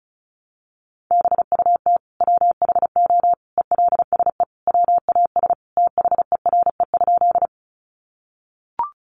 Morse code, heard as a keyed tone, about 34 wpm, 700 Hz. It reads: BUT WHO ELSE WAS THERE?
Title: Sentences from Top 800 Words 35wpm